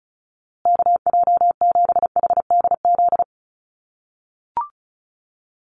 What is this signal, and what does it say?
Morse code, heard as a keyed tone, about 34 wpm, 700 Hz. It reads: KJ7HDZ
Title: USA Call Signs Repeated 3x 35wpm